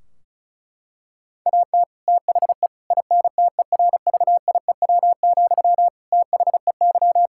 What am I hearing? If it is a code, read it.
Morse code, heard as a keyed tone, about 36 wpm, 700 Hz. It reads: AT THE INTERVIEW, THEY
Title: Sentences from Top 1600 Words 35wpm